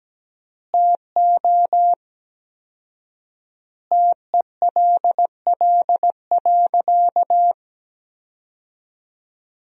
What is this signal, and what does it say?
Morse code, heard as a keyed tone, about 17 wpm, 700 Hz. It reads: TO TELL.
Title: 4x Word-Spacing - Sentences from Top 300 Words - 17wpm